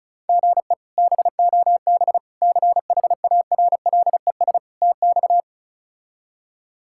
Morse code, heard as a keyed tone, about 36 wpm, 700 Hz. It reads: GE BOB CHARLES TX
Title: K1USN SST QSO Practice - 35wpm